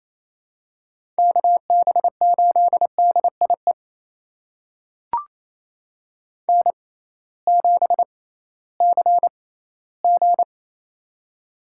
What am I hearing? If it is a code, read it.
Morse code, heard as a keyed tone, about 28 wpm, spaced as for 16 wpm, 700 Hz. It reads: KB8DIE N7CG
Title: USA Call Signs - 6x Character Spacing - 28wpm